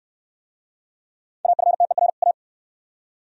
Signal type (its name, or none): Morse code